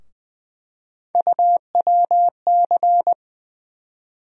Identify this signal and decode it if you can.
Morse code, heard as a keyed tone, about 20 wpm, 700 Hz. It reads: UWC
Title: USA Call Sign Postfixes - 3 Letters 20wpm